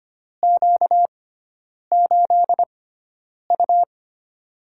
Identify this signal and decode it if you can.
Morse code, heard as a keyed tone, about 25 wpm, spaced as for 9 wpm, 700 Hz. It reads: Q8U
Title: USA Call Signs - 6x Character Spacing - 25wpm